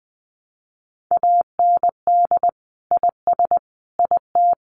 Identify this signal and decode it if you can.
Morse code, heard as a keyed tone, about 20 wpm, 700 Hz. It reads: AND IS IT
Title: Sentences from Top 600 Words 20wpm